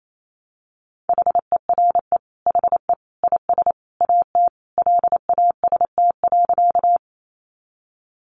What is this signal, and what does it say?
Morse code, heard as a keyed tone, about 28 wpm, 700 Hz. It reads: HERE HE IS AT LAST.
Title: Sentences from Top 300 Words 28wpm